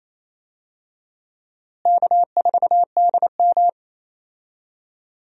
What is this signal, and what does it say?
Morse code, heard as a keyed tone, about 28 wpm, 700 Hz. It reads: K4DM
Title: USA Call Signs - 4 Characters - Repeated 2x - 28wpm